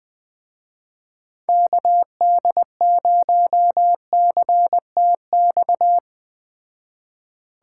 Morse code, heard as a keyed tone, about 21 wpm, 700 Hz. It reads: KD0CTX